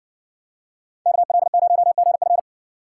Morse code, heard as a keyed tone, about 60 wpm, 700 Hz. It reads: KL0GF